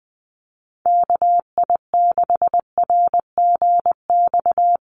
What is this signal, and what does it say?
Morse code, heard as a keyed tone, about 20 wpm, 700 Hz. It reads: KI6RGX
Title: USA Call Signs Repeated 2x 20wpm